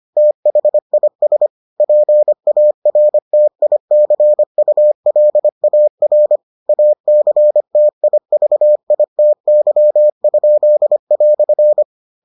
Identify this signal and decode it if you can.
Morse code, heard as a keyed tone, about 25 wpm, 600 Hz. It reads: THIS PARTICULAR ACTIVITY?"